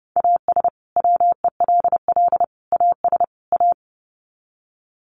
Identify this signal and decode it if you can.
Morse code, heard as a keyed tone, about 30 wpm, 700 Hz. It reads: AS WELL AS A
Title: Sets of 4 Words 30wpm